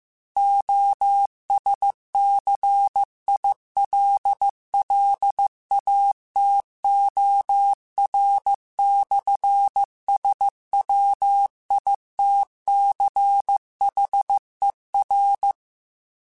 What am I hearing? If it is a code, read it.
Morse code, heard as a keyed tone, about 15 wpm, 800 Hz. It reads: OSCILLATOR/SWITCHER
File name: Technical Subjects / Technical Subjects-mp3s / Sw Pwr Sup @13